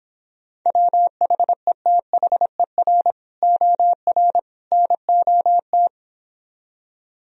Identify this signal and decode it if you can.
Morse code, heard as a keyed tone, about 26 wpm, 700 Hz. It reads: WHETHER OR NOT